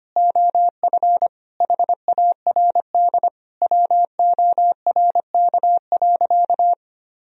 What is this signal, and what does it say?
Morse code, heard as a keyed tone, about 25 wpm, 700 Hz. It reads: OF HARD WORK.
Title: Sentences from Top 1000 Words - Review Part 2 - 25wpm